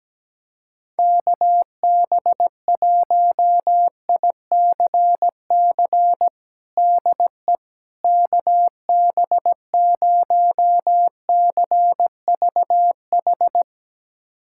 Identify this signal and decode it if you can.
Morse code, heard as a keyed tone, about 17 wpm, 700 Hz. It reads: KB1ICC DE KB0CVH
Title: USA Callsign DE Callsign 17wpm